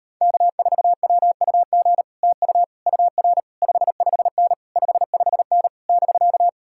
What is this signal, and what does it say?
Morse code, heard as a keyed tone, about 39 wpm, 700 Hz. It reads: K4WUG TU UR 55N 55N <BK>